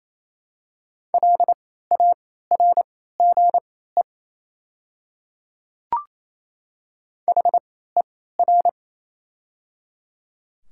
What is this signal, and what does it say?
Morse code, heard as a keyed tone, about 28 wpm, spaced as for 16 wpm, 700 Hz. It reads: LARGE HER
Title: 3x Character Spacing - Top 200 Words - 28wpm